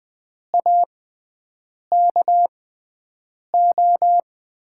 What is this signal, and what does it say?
Morse code, heard as a keyed tone, about 20 wpm, spaced as for 7 wpm, 700 Hz. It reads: AKO
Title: USA Call Signs - 6x Character Spacing - 20wpm